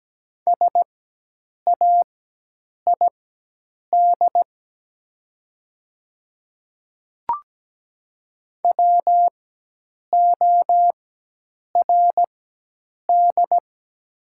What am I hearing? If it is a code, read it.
Morse code, heard as a keyed tone, about 17 wpm, spaced as for 8 wpm, 700 Hz. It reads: SAID WORD
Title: 4x Character Spacing - Top 200 Words - 17wpm